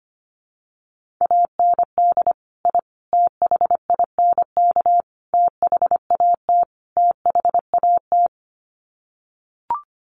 Morse code, heard as a keyed tone, about 25 wpm, 700 Hz. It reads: AND I THINK THAT THAT
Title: Sets of 5 Words Encore 25wpm